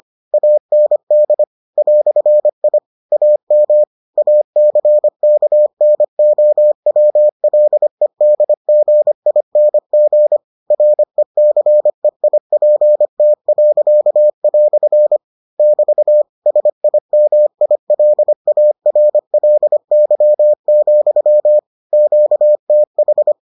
Morse code, heard as a keyed tone, about 25 wpm, 600 Hz. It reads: AND "I AM ACKNOWLEDGING RECEIPT." <BT> SIMILARLY, QTH